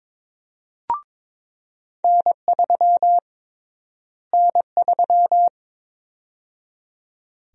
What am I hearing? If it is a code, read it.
Morse code, heard as a keyed tone, about 22 wpm, 700 Hz. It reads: N3 N3
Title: USA Call Sign Prefixes - Repeated 2x - 22wpm